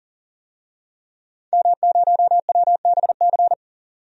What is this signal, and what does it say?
Morse code, heard as a keyed tone, about 40 wpm, 700 Hz. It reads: M0WBC
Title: DXCC-100 Call Signs - 5 Characters - 40wpm